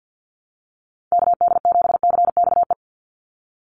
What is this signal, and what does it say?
Morse code, heard as a keyed tone, about 50 wpm, 700 Hz. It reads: KD7CXE
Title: USA Call Signs 50wpm